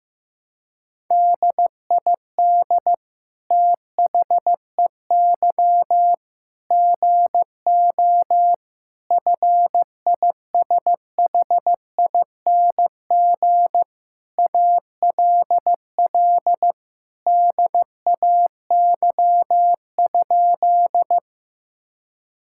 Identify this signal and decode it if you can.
Morse code, heard as a keyed tone, about 15 wpm, 700 Hz. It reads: DID THEY GO FISHING ALL DAY?
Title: Sentences from Top 1000 Words - Review Part 5 - 15wpm